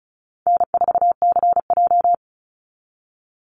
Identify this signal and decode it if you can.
Morse code, heard as a keyed tone, about 34 wpm, 700 Hz. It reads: N4CJ
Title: Easy Call Signs Repeated 2x - 35wpm